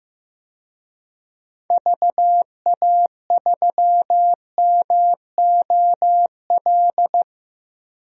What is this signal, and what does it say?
Morse code, heard as a keyed tone, about 15 wpm, 700 Hz. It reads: VA3MOL
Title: DXCC-100 Call Signs - 6 Characters - 15wpm